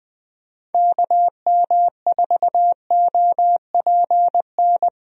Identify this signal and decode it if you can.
Morse code, heard as a keyed tone, about 20 wpm, 700 Hz. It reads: KM4OPN
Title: USA Call Signs - 6 Characters - Repeated 2x - 20wpm